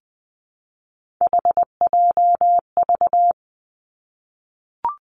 Morse code, heard as a keyed tone, about 20 wpm, 700 Hz. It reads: HJV
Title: Three Challenging Letter-Number Combinations 20wpm